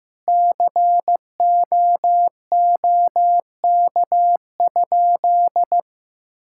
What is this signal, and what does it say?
Morse code, heard as a keyed tone, about 15 wpm, 700 Hz. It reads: COOK?
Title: Sentences from Top 1000 Words - Review Part 6 - 15wpm